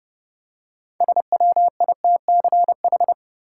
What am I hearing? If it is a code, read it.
Morse code, heard as a keyed tone, about 30 wpm, 700 Hz. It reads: SWITCH
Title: Six Letter Words - 30wpm